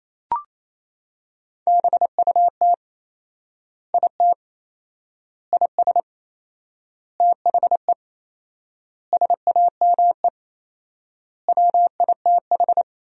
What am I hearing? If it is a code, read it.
Morse code, heard as a keyed tone, about 28 wpm, 700 Hz. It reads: BUT IT IS THE SAME WITH
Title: 4x Word-Spacing - Sentences from Top 300 Words - 28wpm